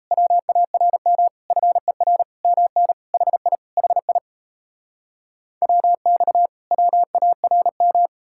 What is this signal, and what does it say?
Morse code, heard as a keyed tone, about 40 wpm, 700 Hz. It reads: WARM FER MN HI HI WX WARM